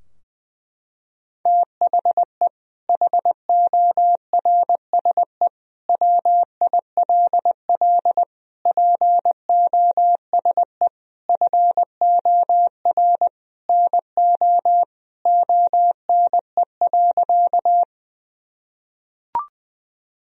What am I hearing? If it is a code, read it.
Morse code, heard as a keyed tone, about 20 wpm, 700 Hz. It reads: THE HORSE WILL POSE FOR NO ONE.
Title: Sentences from Top 1000 Words - Review Part 2 - 20wpm